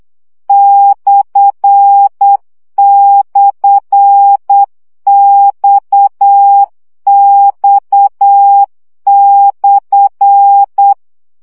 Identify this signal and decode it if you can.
Morse code, heard as a keyed tone, about 8 wpm, 800 Hz. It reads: //XX/
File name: CW gyakorlas x per 7 baud